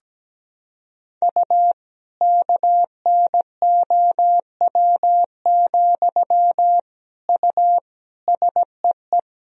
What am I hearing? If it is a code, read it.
Morse code, heard as a keyed tone, about 17 wpm, 700 Hz. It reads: U KNOW, U SEE
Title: Sentences from Top 200 Words and Abbreviations 17wpm